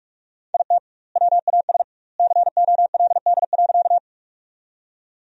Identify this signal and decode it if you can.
Morse code, heard as a keyed tone, about 43 wpm, 700 Hz. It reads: IT WAS COLD.
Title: 2x Word-Spacing - Sentences from Top 400 Words - 45wpm